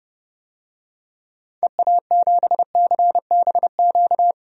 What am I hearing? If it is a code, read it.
Morse code, heard as a keyed tone, about 30 wpm, 700 Hz. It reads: EA7CBQ
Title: DXCC-100 Call Signs - 6 Characters - 30wpm